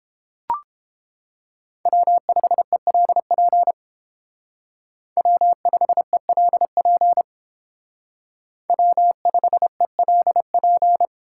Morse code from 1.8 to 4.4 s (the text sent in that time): W5ELP